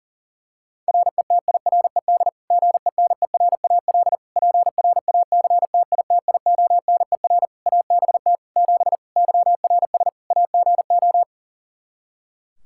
Morse code, heard as a keyed tone, about 40 wpm, 700 Hz. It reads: RETIRED GENERAL PRACTITIONER ABT 7 YRS AGO
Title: Practice QSO - Exchange 3 - More Info - 40wpm